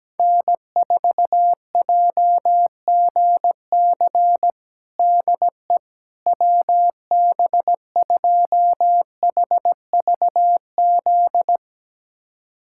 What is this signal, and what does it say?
Morse code, heard as a keyed tone, about 17 wpm, 700 Hz. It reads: N4JGC DE WB2HVZ
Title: USA Callsign DE Callsign 17wpm